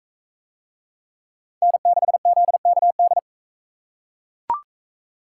Morse code, heard as a keyed tone, about 40 wpm, 700 Hz. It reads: N6ZKD